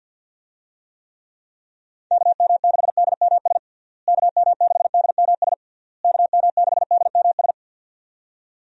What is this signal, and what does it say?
Morse code, heard as a keyed tone, about 52 wpm, 700 Hz. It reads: KM6DMS KM6DMS KM6DMS
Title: USA Call Signs - 6 Characters - Repeated 3x - 50wpm